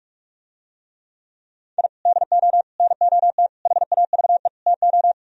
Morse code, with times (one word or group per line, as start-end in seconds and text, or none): 1.8-1.9 I
2.1-2.6 DO
2.8-3.5 NOT
3.7-4.5 HAVE
4.7-5.1 TO